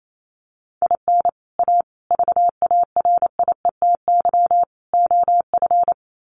Morse code, heard as a keyed tone, about 28 wpm, 700 Hz. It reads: IN A VARIETY OF
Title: Sets of 4 Words 28wpm